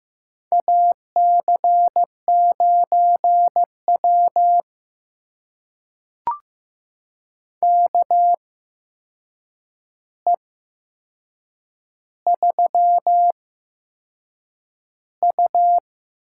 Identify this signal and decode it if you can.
Morse code, heard as a keyed tone, about 15 wpm, spaced as for 6 wpm, 700 Hz. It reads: AC9W KE3U
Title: USA Call Signs - 4 Characters - 8x Character Spacing - 15wpm